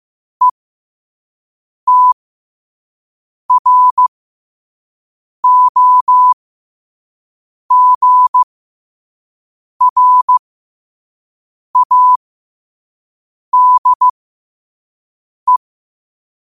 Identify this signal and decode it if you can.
Morse code, heard as a keyed tone, about 15 wpm, spaced as for 5 wpm, 1000 Hz. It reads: ETROGRADE
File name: qotdfiles / 5wpm